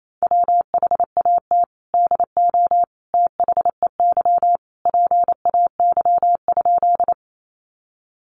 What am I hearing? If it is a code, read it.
Morse code, heard as a keyed tone, about 28 wpm, 700 Hz. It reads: WHAT DO THEY PAY?
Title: Sentences from Top 700 Words 28wpm